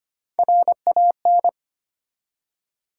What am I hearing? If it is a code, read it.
Morse code, heard as a keyed tone, about 25 wpm, 700 Hz. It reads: RAN
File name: Three Letter-Number Combinations - 25wpm